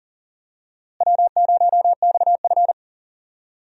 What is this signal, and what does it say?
Morse code, heard as a keyed tone, about 40 wpm, 700 Hz. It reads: W0XF
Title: USA Call Signs - 4 Characters - Repeated 3x - 40wpm